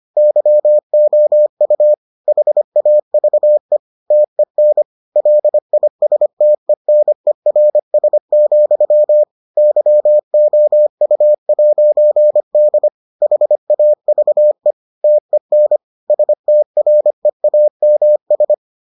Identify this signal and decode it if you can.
Morse code, heard as a keyed tone, about 25 wpm, 600 Hz. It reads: YOU HAVE TEN LISTENERS, YOU'D HAVE TEN STREAMS